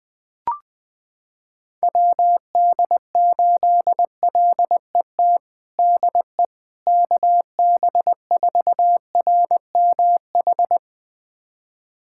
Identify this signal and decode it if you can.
Morse code, heard as a keyed tone, about 20 wpm, 700 Hz. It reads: WD8LET DE KB4RMH
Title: USA Callsign DE Callsign 20wpm